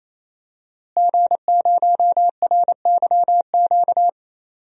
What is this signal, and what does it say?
Morse code, heard as a keyed tone, about 28 wpm, 700 Hz. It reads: G0RYQ
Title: DXCC-100 Call Signs - 5 Characters - Repeated 3x - 28wpm